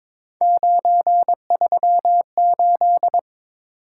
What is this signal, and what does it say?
Morse code, heard as a keyed tone, about 22 wpm, 700 Hz. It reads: 938
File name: practice / 22wpm / Three Letter-Number Combinations - 22wpm